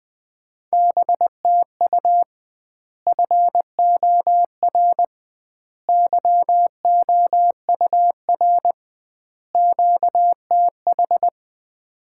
Morse code, heard as a keyed tone, about 20 wpm, 700 Hz. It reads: BTU FOR YOUR QTH
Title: Common QSO Phrases - 2x Word-Spacing - 20wpm